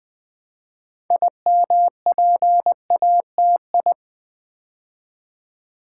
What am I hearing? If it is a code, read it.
Morse code, heard as a keyed tone, about 20 wpm, 700 Hz. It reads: IMPATI
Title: Build A Word - 3 Letters At A Time - 9-Letter Words - 20wpm